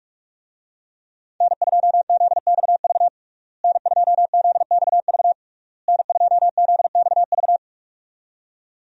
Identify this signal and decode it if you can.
Morse code, heard as a keyed tone, about 47 wpm, 700 Hz. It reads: N2ZXV N2ZXV N2ZXV
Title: USA Call Signs - 5 Characters - Repeated 3x - 45wpm